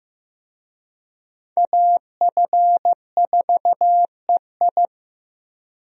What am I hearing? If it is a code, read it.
Morse code, heard as a keyed tone, about 15 wpm, 700 Hz. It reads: AF4EI
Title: USA Call Signs - 5 Characters - 15wpm